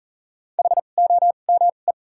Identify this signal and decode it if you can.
Morse code, heard as a keyed tone, about 40 wpm, spaced as for 29 wpm, 700 Hz. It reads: HOME
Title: 2x Character Spacing - Top 300 Words - 40wpm